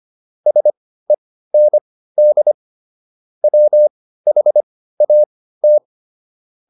Morse code, heard as a keyed tone, about 25 wpm, spaced as for 15 wpm, 600 Hz. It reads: SEND WHAT